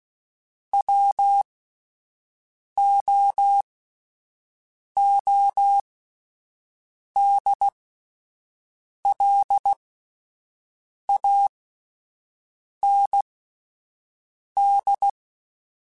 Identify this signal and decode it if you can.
Morse code, heard as a keyed tone, about 16 wpm, spaced as for 5 wpm, 800 Hz. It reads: WOODLAND